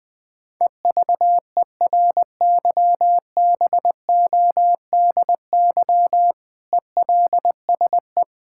Morse code, heard as a keyed tone, about 20 wpm, 700 Hz. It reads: EVERYBODY ELSE
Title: Sentences from Top 1600 Words 20wpm